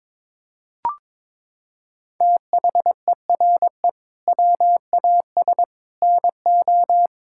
Morse code, heard as a keyed tone, about 22 wpm, 700 Hz. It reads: THERE WAS NO
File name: practice / 22wpm / Sentences with 3-Word Phrases - 22wpm